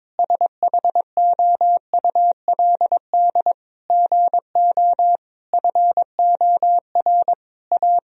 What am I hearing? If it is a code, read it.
Morse code, heard as a keyed tone, about 22 wpm, 700 Hz. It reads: SHOULD GO FOR A
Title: Sentences from Top 1000 Words - Review Part 2 - 22wpm